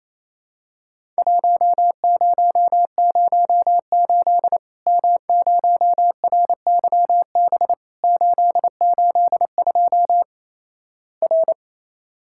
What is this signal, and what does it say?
Morse code, heard as a keyed tone, about 28 wpm, 700 Hz. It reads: 1008 M0RYB 882 R
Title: Worked All Europe DX Contest - QTC Exchange - 28wpm